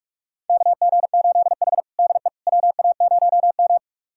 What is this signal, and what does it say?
Morse code, heard as a keyed tone, about 43 wpm, 700 Hz. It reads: KG8H DE WA0M
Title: USA Callsign DE Callsign - 4 Characters - 45wpm